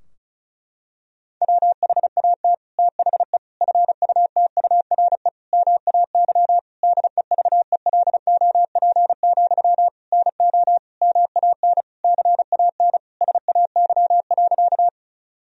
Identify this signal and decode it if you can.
Morse code, heard as a keyed tone, about 34 wpm, 700 Hz. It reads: WHAT THE FUTURE MAY DEVELOP, NO MAN CAN SAY.
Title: Sentences from Top 1000 Words - Review Part 2 - 35wpm